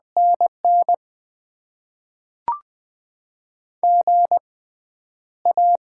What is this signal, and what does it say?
Morse code, heard as a keyed tone, about 20 wpm, spaced as for 10 wpm, 690 Hz. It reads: NN GA
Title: Top 20 QSO Elements - 6x Character Spacing - 20wpm